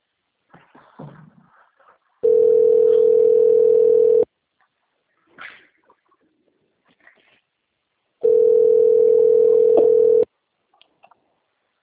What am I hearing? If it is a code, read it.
Ringback tone (North America), 440 + 480 Hz.
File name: out- - - - - .56999